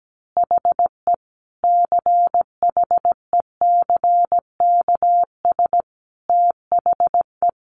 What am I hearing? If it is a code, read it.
Morse code, heard as a keyed tone, about 17 wpm, 700 Hz. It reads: HE CHECKS THE
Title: Easy Phrases 17wpm